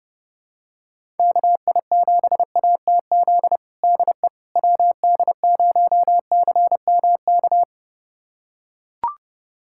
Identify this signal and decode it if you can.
Morse code, heard as a keyed tone, about 32 wpm, 700 Hz. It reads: KI7ATZ DE WD0CMK
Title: USA Callsign DE Callsign - 6 Characters - 30wpm